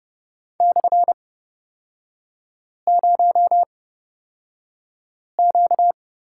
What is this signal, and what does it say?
Morse code, heard as a keyed tone, about 30 wpm, spaced as for 5 wpm, 700 Hz. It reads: /0Q
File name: Lesson 38e - Letter q - Three Characters 30wpm